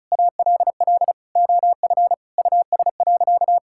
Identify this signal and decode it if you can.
Morse code, heard as a keyed tone, about 34 wpm, 700 Hz. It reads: ALL OF US.
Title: Sentences from Top 1000 Words - Review Part 5 - 35wpm